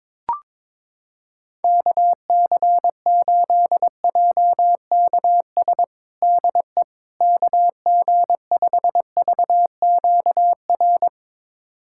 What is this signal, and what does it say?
Morse code, heard as a keyed tone, about 22 wpm, 700 Hz. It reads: KC8JKS DE KG5VQR
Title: USA Callsign DE Callsign - 6 Characters - 22wpm